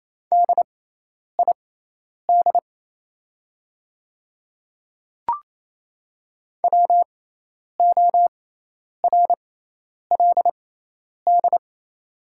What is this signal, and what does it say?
Morse code, heard as a keyed tone, about 28 wpm, spaced as for 10 wpm, 700 Hz. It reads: DID WORLD
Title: 6x Character Spacing - Top 300 Words - 28wpm